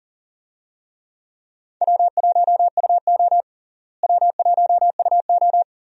Morse code, heard as a keyed tone, about 37 wpm, 700 Hz. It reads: W1UO W1UO
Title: USA Call Signs - 4 Characters - Repeated 2x - 40wpm